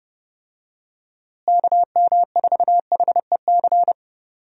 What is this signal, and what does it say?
Morse code, heard as a keyed tone, about 30 wpm, 700 Hz. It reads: KM4HEC
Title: USA Call Signs - 6 Characters - Repeated 2x - 30wpm